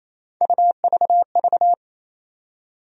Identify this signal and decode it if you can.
Morse code, heard as a keyed tone, about 28 wpm, 700 Hz. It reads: UVV
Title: USA Call Sign Postfixes - 3 Letters 28wpm